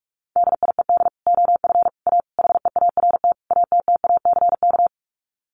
Morse code, heard as a keyed tone, about 44 wpm, 700 Hz. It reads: DIED OF A HEART ATTACK
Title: Sets of 5 Words Encore2 45wpm